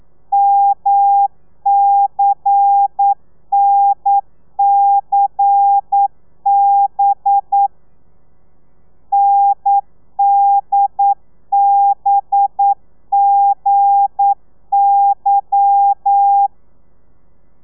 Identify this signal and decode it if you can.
Morse code, heard as a keyed tone, about 9 wpm, 800 Hz. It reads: MCNCB NDBGY
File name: telegrafia / i2tfj / Lez11 40